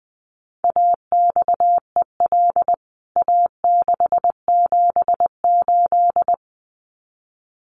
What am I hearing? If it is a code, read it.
Morse code, heard as a keyed tone, about 20 wpm, 700 Hz. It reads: AXEL A678
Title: CWT Exchange 20wpm